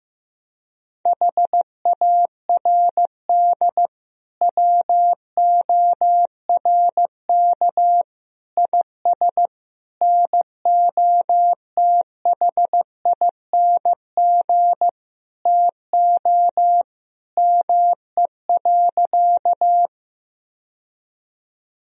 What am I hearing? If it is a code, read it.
Morse code, heard as a keyed tone, about 15 wpm, 700 Hz. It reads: HARD WORK IS NOTHING TO ME.
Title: Sentences from Top 1000 Words - Review Part 1 - 15wpm